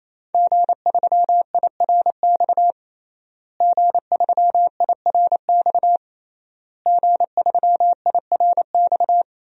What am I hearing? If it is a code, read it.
Morse code, heard as a keyed tone, about 28 wpm, 700 Hz. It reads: G3IRX G3IRX G3IRX
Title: DXCC-100 Call Signs - 5 Characters - Repeated 3x - 28wpm